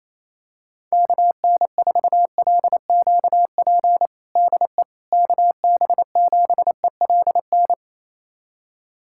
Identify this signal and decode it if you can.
Morse code, heard as a keyed tone, about 28 wpm, 700 Hz. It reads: KN4LQP DE KB7ELN
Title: USA Callsign DE Callsign 28wpm